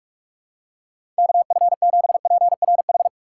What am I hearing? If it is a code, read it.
Morse code, heard as a keyed tone, about 46 wpm, 700 Hz. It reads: KF7PRH
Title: USA Call Signs - 6x Character Spacing - 45wpm